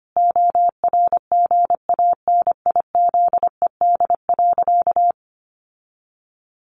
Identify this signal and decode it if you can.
Morse code, heard as a keyed tone, about 25 wpm, 700 Hz. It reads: ORGANIZED.
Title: Sentences from Top 1300 Words 25wpm